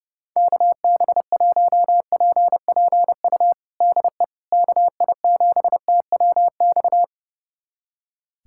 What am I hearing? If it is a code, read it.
Morse code, heard as a keyed tone, about 32 wpm, 700 Hz. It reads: KB1PPU DE KI7TWX
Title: USA Callsign DE Callsign - 6 Characters - 30wpm